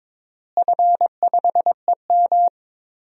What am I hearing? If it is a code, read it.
Morse code, heard as a keyed tone, about 22 wpm, 700 Hz. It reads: F5EM
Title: DXCC-100 Call Signs - 4 Characters - Repeated 3x - 22wpm